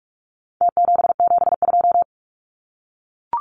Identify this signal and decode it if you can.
Morse code, heard as a keyed tone, about 44 wpm, 700 Hz. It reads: T772
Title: Numbers - Contest Style - 4-Digit 45wpm